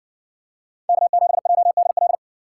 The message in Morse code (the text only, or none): K7JDL